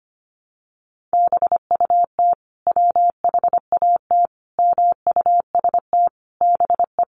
Morse code, heard as a keyed tone, about 25 wpm, 700 Hz. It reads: BUT WHAT MUST BE
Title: Sentences from Top 500 Words 25wpm